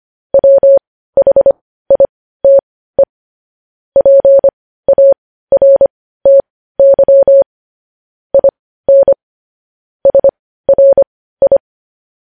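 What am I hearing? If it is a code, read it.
Morse code, heard as a keyed tone, about 25 wpm, spaced as for 15 wpm, 550 Hz. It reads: WHITE PARTY IN SRI
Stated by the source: News headlines sent as Morse practice at 15.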